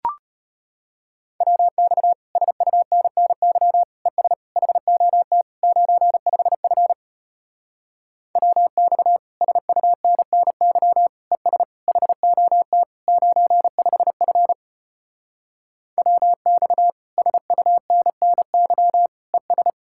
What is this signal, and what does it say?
Morse code, heard as a keyed tone, about 33 wpm, 700 Hz. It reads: WX SUNNY ES HOT 95F WX SUNNY ES HOT 95F WX SUNNY ES